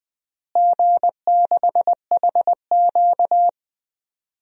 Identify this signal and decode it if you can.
Morse code, heard as a keyed tone, about 20 wpm, 700 Hz. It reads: G6HQ
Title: DXCC-100 Call Signs - 4 Characters - Repeated 3x - 20wpm